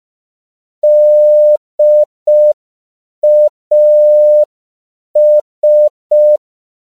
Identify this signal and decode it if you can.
Morse code, heard as a keyed tone, about 5 wpm, 600 Hz. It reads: DAS
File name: CW / 5PPM